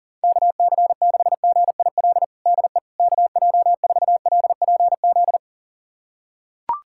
Morse code, heard as a keyed tone, about 40 wpm, 690 Hz. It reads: KC6GIL DE KJ4LPZ
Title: USA Callsign DE Callsign - 6 Characters - 40wpm